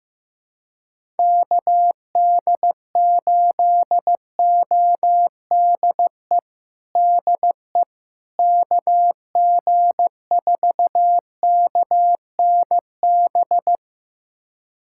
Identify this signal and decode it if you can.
Morse code, heard as a keyed tone, about 15 wpm, 700 Hz. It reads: KD8ODE DE KG4KNB
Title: USA Callsign DE Callsign - 6 Characters - 15wpm